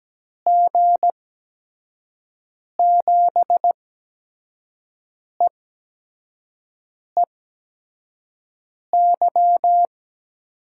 Morse code, heard as a keyed tone, about 17 wpm, spaced as for 5 wpm, 700 Hz. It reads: G7EEY
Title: USA Call Signs - 8x Character Spacing - 17wpm